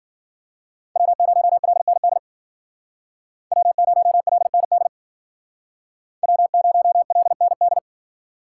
Morse code, heard as a keyed tone, about 52 wpm, 700 Hz. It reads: W0LND W0LND W0LND